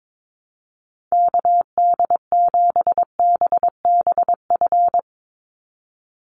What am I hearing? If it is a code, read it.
Morse code, heard as a keyed tone, about 22 wpm, 700 Hz. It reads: KD7BBF
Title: USA Call Signs 22wpm